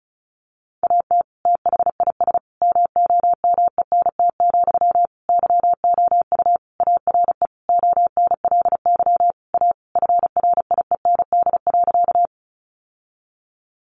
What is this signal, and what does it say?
Morse code, heard as a keyed tone, about 34 wpm, 700 Hz. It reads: AT THIS MOMENT, YOU ARE ONLY A FRIEND.
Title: Sentences from Top 1000 Words 35wpm